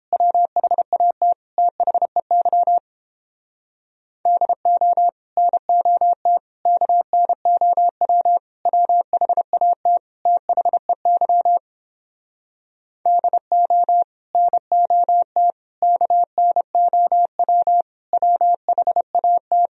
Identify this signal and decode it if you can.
Morse code, heard as a keyed tone, about 29 wpm, 700 Hz. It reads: WHAT THEY DO NOT KNOW WHAT THEY DO NOT KNOW WHAT